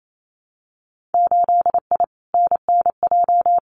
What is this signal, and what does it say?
Morse code, heard as a keyed tone, about 28 wpm, 700 Hz. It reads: 8I NNJ
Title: Winter Field Day Exchanges Encore 28wpm